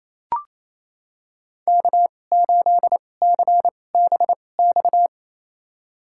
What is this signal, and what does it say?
Morse code, heard as a keyed tone, about 28 wpm, spaced as for 20 wpm, 700 Hz. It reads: K8CBX
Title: USA Call Signs - 5 Characters - 2x Character Spacing - 28wpm